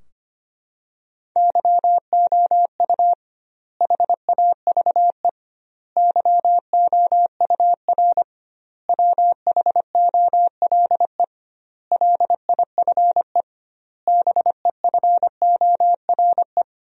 Morse code, heard as a keyed tone, about 25 wpm, 700 Hz. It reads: YOU HAVE YOUR WHOLE LIFE BEFORE
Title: 2x Word-Spacing - Sentences from Top 400 Words - 25wpm